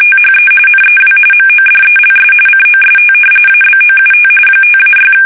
RTTY, about 45.45 baud. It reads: THE NEW MODELS COST ABOUT 5-6
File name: fsk850